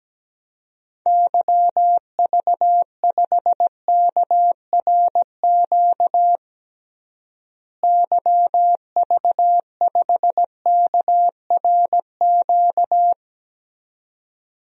Morse code, heard as a keyed tone, about 17 wpm, 700 Hz. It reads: YV5KRQ YV5KRQ
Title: DXCC-100 Call Signs - 6 Characters - Repeated 2x - 17wpm